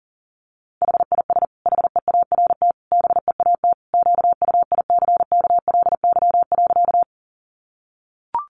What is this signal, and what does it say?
Morse code, heard as a keyed tone, about 40 wpm, 700 Hz. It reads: HIS HEART BEAT QUICKLY.
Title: Sentences from Top 1300 Words 40wpm